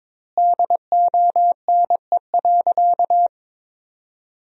Morse code, heard as a keyed tone, about 23 wpm, 700 Hz. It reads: DONE.
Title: 4x Word-Spacing - Sentences from Top 500 Words - 22wpm